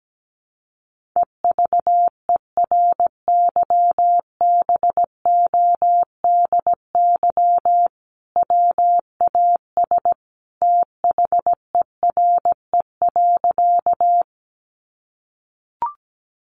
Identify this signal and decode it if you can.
Morse code, heard as a keyed tone, about 17 wpm, 700 Hz. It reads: EVERYBODY WAS THERE.
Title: Sentences from Top 1400 Words 17wpm